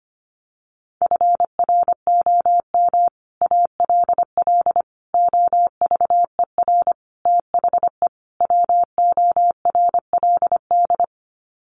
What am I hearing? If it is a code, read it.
Morse code, heard as a keyed tone, about 25 wpm, 700 Hz. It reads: FROM ALL OVER THE WORLD
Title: Sets of 5 Words 25wpm